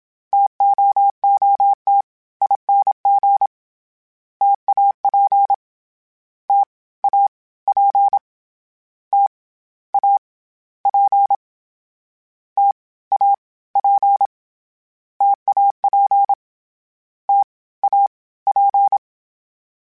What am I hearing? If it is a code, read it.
Morse code, heard as a keyed tone, about 27 wpm, 800 Hz. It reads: TOOTING TAP T A P T A P T A P TAP T A P
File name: Difficult Combos / Difficult Combos-mp3s / T AOEI @25